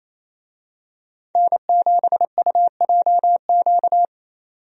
Morse code, heard as a keyed tone, about 28 wpm, 700 Hz. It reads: N7UJQ
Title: USA Call Signs - 5 Characters - 28wpm